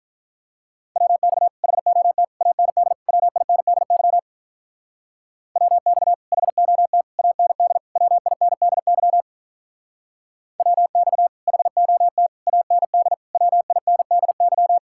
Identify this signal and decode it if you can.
Morse code, heard as a keyed tone, about 44 wpm, 700 Hz. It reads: WX HOT AND WINDY WX HOT AND WINDY WX HOT AND WINDY
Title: Speed-Racing Common QSO Phrases - 35wpm